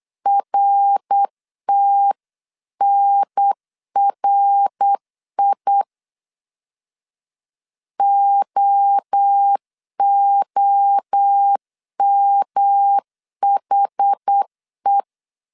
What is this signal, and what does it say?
Morse code, heard as a keyed tone, about 8 wpm, 800 Hz. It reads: RTNRI OOMHE